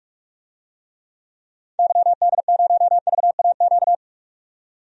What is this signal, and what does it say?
Morse code, heard as a keyed tone, about 42 wpm, 700 Hz. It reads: YD0VAQ
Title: DXCC-100 Call Signs - 6 Characters - 45wpm